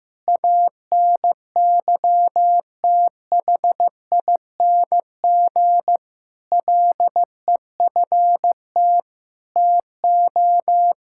Morse code, heard as a keyed tone, about 15 wpm, 700 Hz. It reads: ANYTHING LEFT TO
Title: Sentences from Top 1000 Words - Review Part 6 - 15wpm